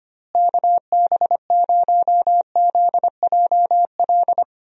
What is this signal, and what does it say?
Morse code, heard as a keyed tone, about 25 wpm, 700 Hz. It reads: KB0ZJL
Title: USA Call Signs - 6 Characters - Repeated 2x - 25wpm